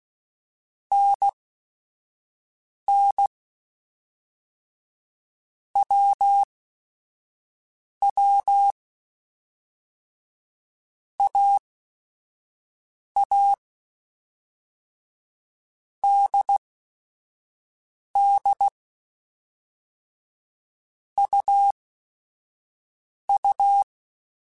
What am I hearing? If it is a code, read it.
Morse code, heard as a keyed tone, about 16 wpm, spaced as for 5 wpm, 800 Hz. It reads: NN WW AA DD UU